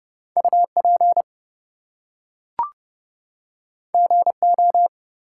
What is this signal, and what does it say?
Morse code, heard as a keyed tone, about 31 wpm, 700 Hz. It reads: UP GO
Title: Two Letter Words - 30wpm